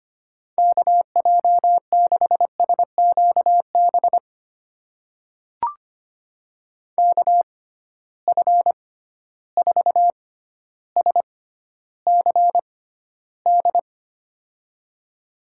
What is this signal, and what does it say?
Morse code, heard as a keyed tone, about 25 wpm, spaced as for 13 wpm, 700 Hz. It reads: KJ6SQB KF4SCD
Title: USA Call Signs - 6 Characters - 6x Character Spacing - 25wpm